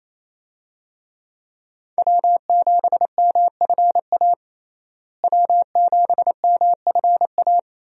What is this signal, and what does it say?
Morse code, heard as a keyed tone, about 28 wpm, 700 Hz. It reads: W7MFA W7MFA
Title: USA Call Signs - 5 Characters - Repeated 2x - 28wpm